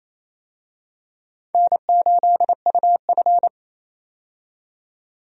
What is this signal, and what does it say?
Morse code, heard as a keyed tone, about 28 wpm, 700 Hz. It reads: N8UF
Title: USA Call Signs - 4 Characters - 28wpm